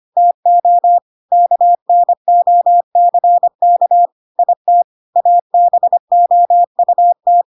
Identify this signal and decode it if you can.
Morse code, heard as a keyed tone, about 25 wpm, 700 Hz. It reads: TO KNOCK IT ABOUT
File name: Pinocchio / Pinocchio ch1 25 25 ik1hge